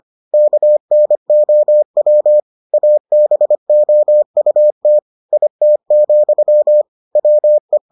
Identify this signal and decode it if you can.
Morse code, heard as a keyed tone, about 25 wpm, 600 Hz. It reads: KNOW ABOUT IT, WE